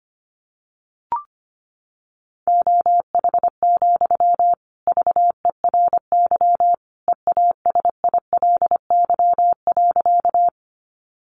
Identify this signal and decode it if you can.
Morse code, heard as a keyed tone, about 25 wpm, 700 Hz. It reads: OH, VERY EASILY.
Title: Sentences from Top 900 Words 25wpm